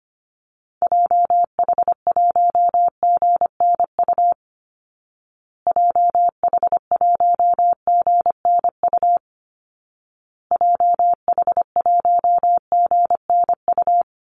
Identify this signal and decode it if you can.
Morse code, heard as a keyed tone, about 25 wpm, 700 Hz. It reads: JH1GNU JH1GNU JH1GNU
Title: Call Signs Repeated 2x - 25wpm